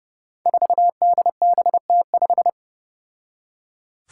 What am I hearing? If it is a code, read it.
Morse code, heard as a keyed tone, about 32 wpm, 700 Hz. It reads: 4DBT5
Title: Five Challenging Letter-Number Combinations 30wpm